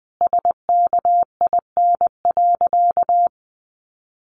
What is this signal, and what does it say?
Morse code, heard as a keyed tone, about 20 wpm, 700 Hz. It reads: SKIN.
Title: Sentences from Top 1300 Words 20wpm